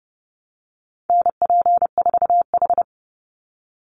Morse code, heard as a keyed tone, about 30 wpm, 700 Hz. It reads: NP4H
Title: Call Signs Repeated 2x - 30wpm